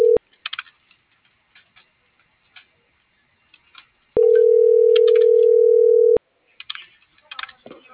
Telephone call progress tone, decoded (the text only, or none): ringback tone